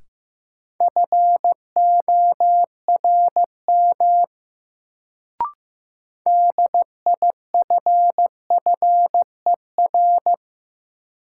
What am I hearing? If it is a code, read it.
Morse code, heard as a keyed tone, about 15 wpm, 700 Hz. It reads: FORM DIFFER
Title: Sentences from Top 1000 Words - Review Part 1 - New Word Drill - 15wpm